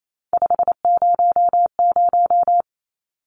Morse code, heard as a keyed tone, about 28 wpm, 700 Hz. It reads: 500
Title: Numbers - 3-Digit 28wpm